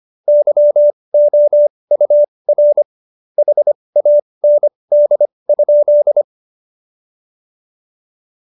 Morse code, heard as a keyed tone, about 25 wpm, spaced as for 20 wpm, 590 Hz. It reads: YOUR HAND?
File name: Jokes-25-char-20-eff